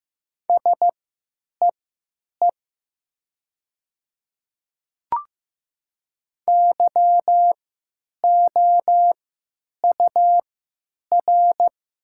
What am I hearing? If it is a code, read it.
Morse code, heard as a keyed tone, about 15 wpm, spaced as for 9 wpm, 700 Hz. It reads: SEE YOUR
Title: 3x Character Spacing - Top 100 Words - 15wpm